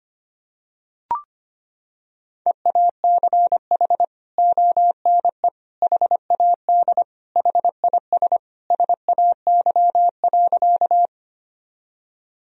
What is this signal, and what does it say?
Morse code, heard as a keyed tone, about 25 wpm, 700 Hz. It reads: EACH ONE HAD HIS SAY.
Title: Sentences from Top 1000 Words - Review Part 1 - 25wpm